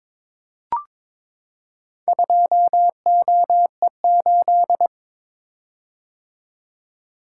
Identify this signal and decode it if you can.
Morse code, heard as a keyed tone, about 22 wpm, 700 Hz. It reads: 2OE8
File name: practice / 22wpm / Four Letter-Number Combinations - 22wpm